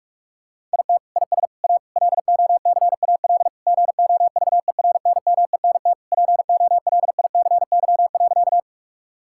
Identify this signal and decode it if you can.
Morse code, heard as a keyed tone, about 46 wpm, 700 Hz. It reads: IT IS A LOCAL GOVERNMENT POLICY.
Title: Sentences from Top 1000 Words - Review Part 5 - 45wpm